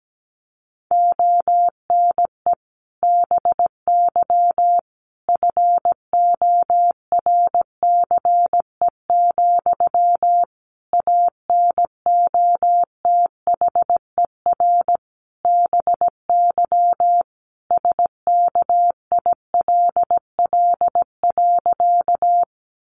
Morse code, heard as a keyed tone, about 17 wpm, 700 Hz. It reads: ONE BY FORCE, ANOTHER BY SKILL.
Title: Sentences from Top 1200 Words 17wpm